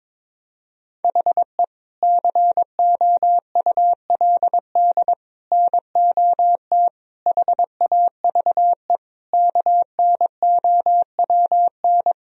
Morse code, heard as a keyed tone, about 22 wpm, 700 Hz. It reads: HE COULD NOT HAVE KNOWN
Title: Sentences from Top 1000 Words - Review Part 3 - 22wpm